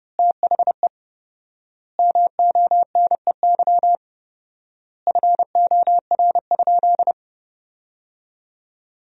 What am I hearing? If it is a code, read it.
Morse code, heard as a keyed tone, about 30 wpm, 700 Hz. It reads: THE MONEY FOR?
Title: 4x Word-Spacing - Sentences from Top 400 Words - 30wpm